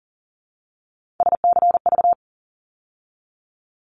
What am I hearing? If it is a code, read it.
Morse code, heard as a keyed tone, about 38 wpm, 700 Hz. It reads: SCV
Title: ARRL Sections 40wpm